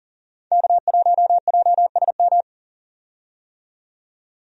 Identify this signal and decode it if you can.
Morse code, heard as a keyed tone, about 40 wpm, 700 Hz. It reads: K1JSM
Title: USA Call Signs - 5 Characters - 40wpm